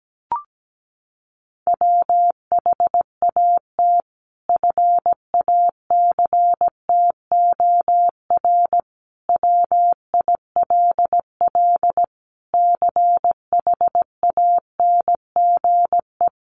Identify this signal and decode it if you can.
Morse code, heard as a keyed tone, about 17 wpm, 700 Hz. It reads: WHAT FACTOR WILL CHANGE
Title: Sentences from Top 1300 Words 17wpm